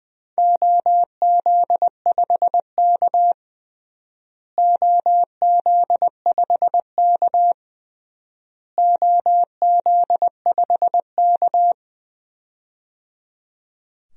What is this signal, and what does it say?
Morse code, heard as a keyed tone, about 20 wpm, 700 Hz. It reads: OZ5K OZ5K OZ5K
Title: DXCC-100 Call Signs - 4 Characters - Repeated 3x - 20wpm